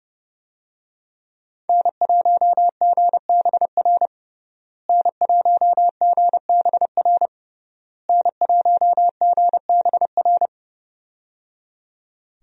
Morse code, heard as a keyed tone, about 30 wpm, 700 Hz. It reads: N1GBR N1GBR N1GBR
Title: USA Call Signs - 5 Characters - Repeated 3x - 30wpm